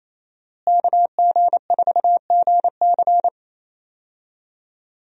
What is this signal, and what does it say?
Morse code, heard as a keyed tone, about 27 wpm, 700 Hz. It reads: KG4GC
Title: USA Call Signs - 5 Characters - Repeated 3x - 28wpm